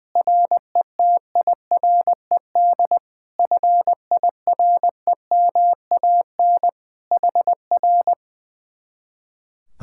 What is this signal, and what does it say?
Morse code, heard as a keyed tone, about 20 wpm, 700 Hz. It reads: RETIRED FIREMAN HR
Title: Speed-Racing Common QSO Phrases Encore - 20wpm